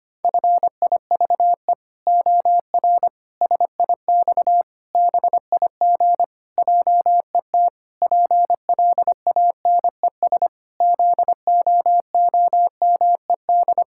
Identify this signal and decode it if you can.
Morse code, heard as a keyed tone, about 25 wpm, 700 Hz. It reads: FIVE OR SIX BIG JET PLANES ZOOMED
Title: Pangram Challenge v2 - 25wpm